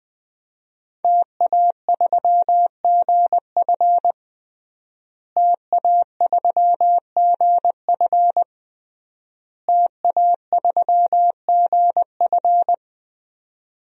Morse code, heard as a keyed tone, about 20 wpm, 700 Hz. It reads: TA3GF TA3GF TA3GF
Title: DXCC-100 Call Signs - 5 Characters - Repeated 3x - 20wpm